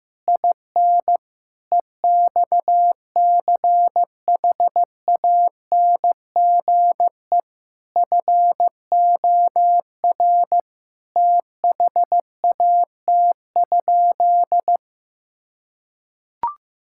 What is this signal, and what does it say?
Morse code, heard as a keyed tone, about 15 wpm, 700 Hz. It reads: IN EXCHANGE FOR THAT?
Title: Sentences from Top 1000 Words - Review Part 6 - 15wpm